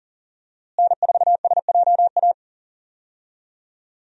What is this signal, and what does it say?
Morse code, heard as a keyed tone, about 40 wpm, 700 Hz. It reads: N4SJA
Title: USA Call Signs - 5 Characters - Repeated 2x - 40wpm